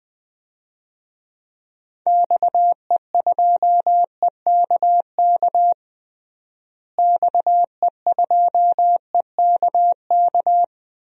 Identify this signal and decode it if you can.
Morse code, heard as a keyed tone, about 21 wpm, 700 Hz. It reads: XE2EKK XE2EKK
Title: DXCC-100 Call Signs - 6 Characters - Repeated 2x - 20wpm